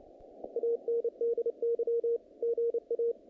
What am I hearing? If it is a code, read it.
Morse code, heard as a keyed tone, about 29 wpm, 440 Hz. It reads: ANDY GA
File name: contests / naqp-cw-k5tr-op-k5pi- -2100